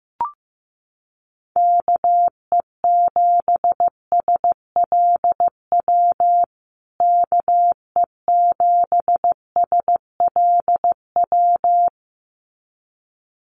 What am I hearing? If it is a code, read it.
Morse code, heard as a keyed tone, about 15 wpm, 700 Hz. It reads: KE7SLW KE7SLW
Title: USA Call Signs Repeated 2x 15wpm